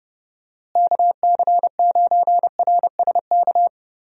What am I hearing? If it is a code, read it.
Morse code, heard as a keyed tone, about 30 wpm, 700 Hz. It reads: KC9RSK
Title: USA Call Signs - 6 Characters - Repeated 2x - 30wpm